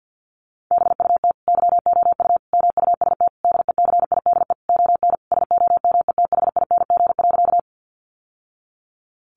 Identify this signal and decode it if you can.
Morse code, heard as a keyed tone, about 48 wpm, 700 Hz. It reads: BUT YOU MUST DECIDE ON SOMETHING.
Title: Sentences from Top 600 Words 50wpm